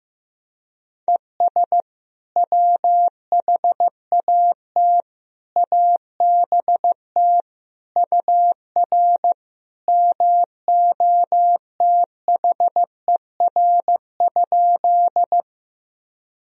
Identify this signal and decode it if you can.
Morse code, heard as a keyed tone, about 15 wpm, 700 Hz. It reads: ES WHAT ABT UR MOTHER?
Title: Sentences from Top 200 Words and Abbreviations 15wpm